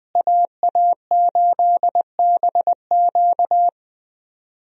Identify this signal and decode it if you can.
Morse code, heard as a keyed tone, about 20 wpm, 700 Hz. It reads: AA8BQ
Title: USA Call Signs - 5 Characters - Repeated 3x - 20wpm